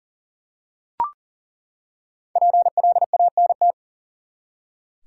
Morse code, heard as a keyed tone, about 40 wpm, 700 Hz. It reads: PLANT
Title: Five Letter Words - 40wpm